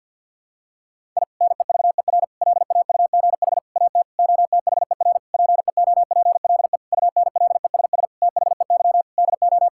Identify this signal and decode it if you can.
Morse code, heard as a keyed tone, about 52 wpm, 700 Hz. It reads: I NEVER LAUGH AT OTHER PEOPLE UNLESS THEY DO
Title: Sentences from Top 1000 Words - Review Part 6 - 50wpm